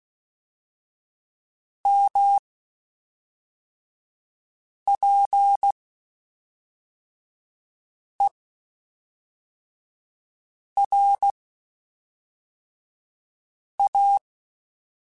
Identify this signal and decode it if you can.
Morse code, heard as a keyed tone, about 16 wpm, 800 Hz. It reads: M P E R A
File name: Spaced Characters / Spaced Characters-mp3s / Weather5 22 sp.@5